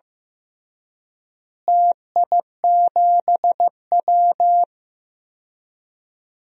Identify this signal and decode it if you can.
Morse code, heard as a keyed tone, about 15 wpm, 700 Hz. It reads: TI7W
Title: DXCC-100 Call Signs - 4 Characters - Repeated 2x - 15wpm